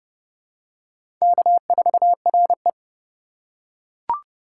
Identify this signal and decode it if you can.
Morse code, heard as a keyed tone, about 31 wpm, 700 Hz. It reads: K4RE
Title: USA Call Signs - 2x Character Spacing - 30wpm